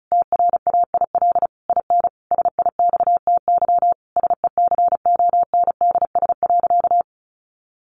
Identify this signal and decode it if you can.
Morse code, heard as a keyed tone, about 34 wpm, 700 Hz. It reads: TRAIL IN SIXTY SECONDS.
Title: A Sound Of Thunder 35wpm